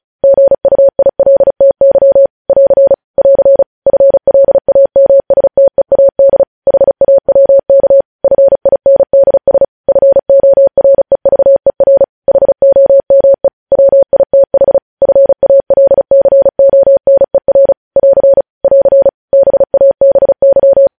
Morse code, heard as a keyed tone, about 36 wpm, 550 Hz. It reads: GUILTY <AR> <AR> FLAMSTEAD HAWK FINDS FOREVER HOME WITH FALCONER <AR> <AR> BABY
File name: podcast / News headlines-1x-35x35